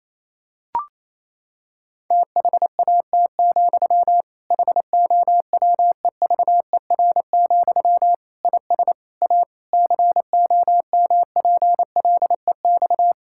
Morse code, heard as a keyed tone, about 28 wpm, 700 Hz. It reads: THAT, HOWEVER, IS A COMPLEX
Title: Sentences from Top 1000 Words - Review Part 6 - 28wpm